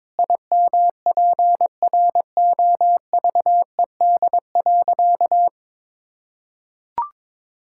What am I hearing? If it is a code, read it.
Morse code, heard as a keyed tone, about 22 wpm, 700 Hz. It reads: IMPROVED.
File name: practice / 22wpm / Sentences with 4-Word Phrases - 22wpm